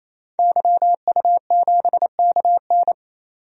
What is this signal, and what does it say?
Morse code, heard as a keyed tone, about 27 wpm, 700 Hz. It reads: YU7KN
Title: DXCC-100 Call Signs - 5 Characters - 28wpm